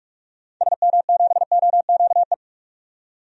Morse code, heard as a keyed tone, about 44 wpm, 690 Hz. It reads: SM7OQE
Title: DXCC-100 Call Signs - 6 Characters - 45wpm